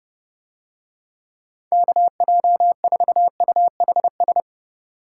Morse code, heard as a keyed tone, about 31 wpm, 700 Hz. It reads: KJ4UHS